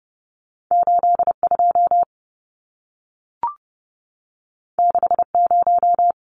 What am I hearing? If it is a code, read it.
Morse code, heard as a keyed tone, about 30 wpm, 700 Hz. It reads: 82 60
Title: Numbers - 2-Digit 30wpm